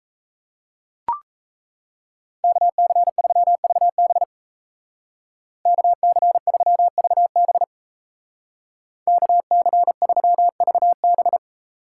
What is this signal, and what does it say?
Morse code, heard as a keyed tone, about 35 wpm, 700 Hz. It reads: KC3VB KC3VB KC3VB
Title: Speed Racing - USA 2x2 Call Signs 28wpm